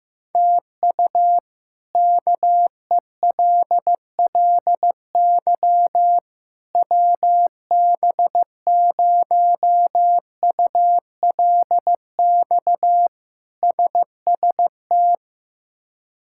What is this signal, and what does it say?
Morse code, heard as a keyed tone, about 15 wpm, 700 Hz. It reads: TU KELLY WB0ULX SST
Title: K1USN SST QSO Practice - 15wpm